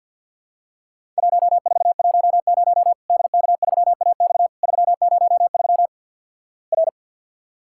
Morse code, heard as a keyed tone, about 50 wpm, 700 Hz. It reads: 1410 DK3AX 303 R
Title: Worked All Europe DX Contest - QTC Exchange - 50wpm